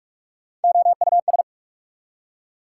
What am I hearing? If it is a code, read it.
Morse code, heard as a keyed tone, about 46 wpm, 690 Hz. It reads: OUS
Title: Prefixes and Suffixes 45wpm